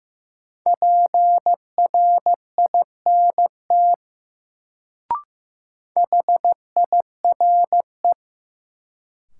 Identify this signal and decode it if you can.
Morse code, heard as a keyed tone, about 15 wpm, 700 Hz. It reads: PRINT HIRE
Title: Sentences from Top 1400 Words - New Word Drill - 15wpm